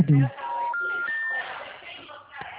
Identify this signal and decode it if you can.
Special information tone, with speech.